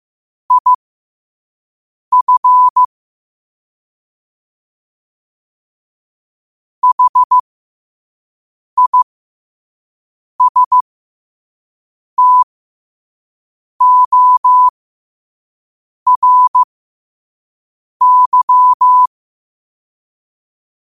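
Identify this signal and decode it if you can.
Morse code, heard as a keyed tone, about 15 wpm, spaced as for 5 wpm, 1000 Hz. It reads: IF HISTORY
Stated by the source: Morse code practice at 5 wpm.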